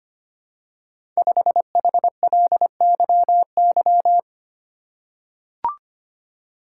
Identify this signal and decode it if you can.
Morse code, heard as a keyed tone, about 25 wpm, 700 Hz. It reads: 5HLYY
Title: Code Groups - 5 Letter:Numbers - 25wpm